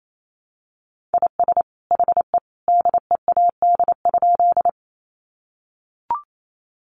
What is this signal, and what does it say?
Morse code, heard as a keyed tone, about 28 wpm, 700 Hz. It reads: IS HE DEAD?
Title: Sentences from Top 1500 Words 28wpm